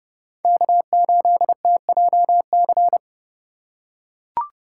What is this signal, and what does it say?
Morse code, heard as a keyed tone, about 30 wpm, 700 Hz. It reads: K8TJC